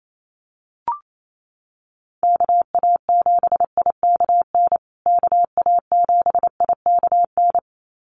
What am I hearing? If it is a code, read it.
Morse code, heard as a keyed tone, about 28 wpm, 700 Hz. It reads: KA7IKN KA7IKN
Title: USA Call Signs Repeated 2x 28wpm